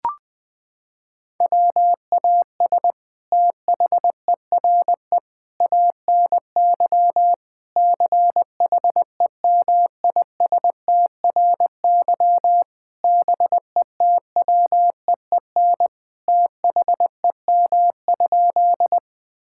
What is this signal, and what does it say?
Morse code, heard as a keyed tone, about 20 wpm, 700 Hz. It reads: WAS THERE ANY CHEMISTRY BETWEEN THEM?
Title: Sentences from Top 1000 Words - Review Part 6 - 20wpm